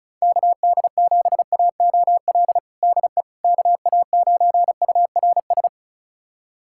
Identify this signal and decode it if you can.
Morse code, heard as a keyed tone, about 36 wpm, 690 Hz. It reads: KD7AOL DE KA9URS
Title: USA Callsign DE Callsign 35wpm